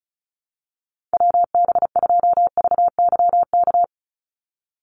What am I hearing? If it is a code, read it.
Morse code, heard as a keyed tone, about 34 wpm, 700 Hz. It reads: WB2VYK
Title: Call Signs Repeated 2x - 35wpm